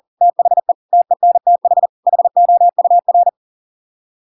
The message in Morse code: THE TENTH HOUR

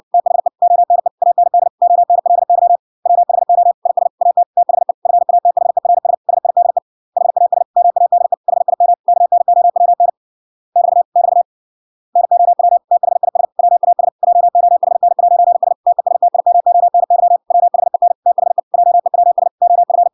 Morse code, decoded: THE ONE AND ONLY WHO IS AT THE FATHERS SIDE HAS MADE HIM KNOWN <BT> <BT> NOW THIS WAS JOHN'S TESTIMONY WHEN THE JEWS OF

